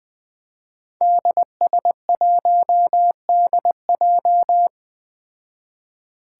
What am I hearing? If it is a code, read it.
Morse code, heard as a keyed tone, about 20 wpm, 700 Hz. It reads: DS1DJ
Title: Five Challenging Letter-Number Combinations 20wpm